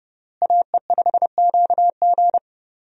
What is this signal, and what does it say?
Morse code, heard as a keyed tone, about 30 wpm, 700 Hz. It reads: AE5QG
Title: USA Call Signs - 5 Characters - Repeated 3x - 30wpm